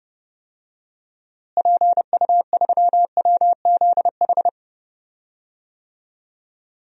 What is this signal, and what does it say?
Morse code, heard as a keyed tone, about 29 wpm, 700 Hz. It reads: PU3WZH
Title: DXCC-100 Call Signs - 6 Characters - Repeated 2x - 30wpm